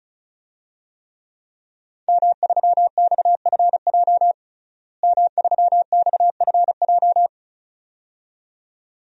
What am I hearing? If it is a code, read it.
Morse code, heard as a keyed tone, about 36 wpm, 700 Hz. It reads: M3XFJ M3XFJ
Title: DXCC-100 Call Signs - 5 Characters - Repeated 2x - 35wpm